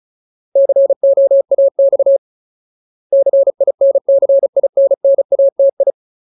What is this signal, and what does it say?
Morse code, heard as a keyed tone, about 34 wpm, 550 Hz. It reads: COAX CINCINNATI
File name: Ultimate Word List / 35WPM Ultimate